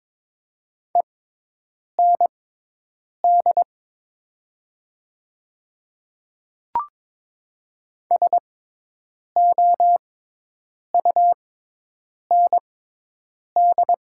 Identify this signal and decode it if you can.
Morse code, heard as a keyed tone, about 21 wpm, spaced as for 8 wpm, 700 Hz. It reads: END SOUND
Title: 6x Character Spacing - Top 200 Words - 22wpm